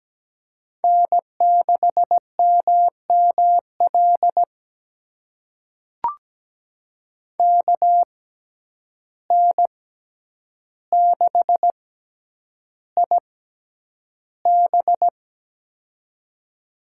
Morse code, heard as a keyed tone, about 17 wpm, spaced as for 9 wpm, 700 Hz. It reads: N6MML KN6IB
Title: USA Call Signs - 5 Characters - 6x Character Spacing - 17wpm